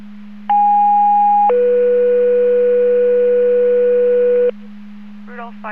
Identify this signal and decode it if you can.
Two-tone page, heard as a one-second tone, then a three-second tone, about 820 then 490 Hz, with speech. It is heard from 0.5 to 4.5 s.